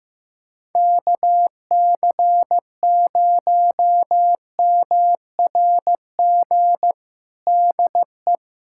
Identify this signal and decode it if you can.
Morse code, heard as a keyed tone, about 15 wpm, 700 Hz. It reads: KC0MRG DE
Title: USA Callsign DE Callsign - 6 Characters - 15wpm